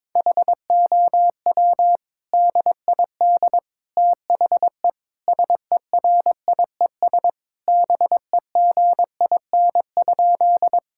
Morse code, heard as a keyed tone, about 22 wpm, 700 Hz. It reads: HOW DID THE SERIES BEGIN?
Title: Sentences from Top 1000 Words - Review Part 5 - 22wpm